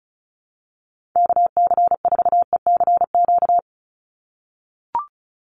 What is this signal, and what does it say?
Morse code, heard as a keyed tone, about 34 wpm, 700 Hz. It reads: KC4ECQ
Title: USA Call Signs 35wpm